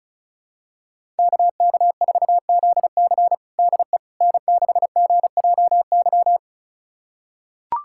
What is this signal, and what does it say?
Morse code, heard as a keyed tone, about 34 wpm, 700 Hz. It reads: KK4ZC DE N6GJY
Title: USA Callsign DE Callsign - 5 Characters - 35wpm